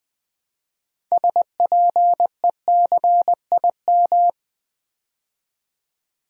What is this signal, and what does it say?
Morse code, heard as a keyed tone, about 20 wpm, 700 Hz. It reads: SPECIM
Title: Build A Word - 2 Letters At A Time - 8-Letter Words - 20wpm